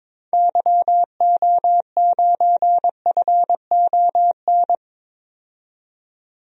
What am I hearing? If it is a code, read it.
Morse code, heard as a keyed tone, about 22 wpm, 700 Hz. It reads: YO9FON
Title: DXCC-100 Call Signs - 6 Characters - 22wpm